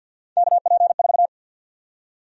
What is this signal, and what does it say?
Morse code, heard as a keyed tone, about 51 wpm, 700 Hz. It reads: KP4
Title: USA Call Sign Prefixes - 50wpm